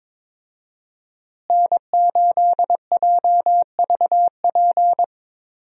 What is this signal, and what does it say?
Morse code, heard as a keyed tone, about 21 wpm, 700 Hz. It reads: N8JVP
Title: USA Call Signs - 5 Characters - Repeated 2x - 22wpm